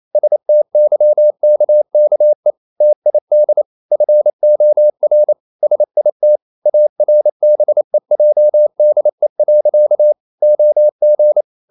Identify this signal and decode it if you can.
Morse code, heard as a keyed tone, about 28 wpm, 600 Hz. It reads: STYKKE TID FOR SIT ARBEJDE. OG
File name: Ferie QRQ 28wpm